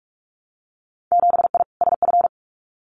Morse code, heard as a keyed tone, about 44 wpm, 700 Hz. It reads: 7I SF
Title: Winter Field Day Exchanges Encore 45wpm